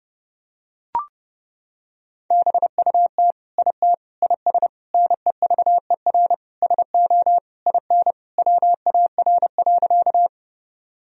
Morse code, heard as a keyed tone, about 30 wpm, 700 Hz. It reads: BUT IT IS NEVER SO IN WAR.
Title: Sentences from Top 1000 Words - Review Part 2 - 30wpm